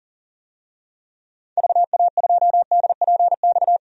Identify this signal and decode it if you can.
Morse code, heard as a keyed tone, about 37 wpm, 700 Hz. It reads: VA2DPX
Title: DXCC-100 Call Signs - 6 Characters - 40wpm